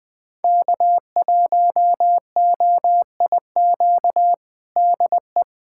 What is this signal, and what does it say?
Morse code, heard as a keyed tone, about 21 wpm, 700 Hz. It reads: K1OIQ DE
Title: USA Callsign DE Callsign - 5 Characters - 20wpm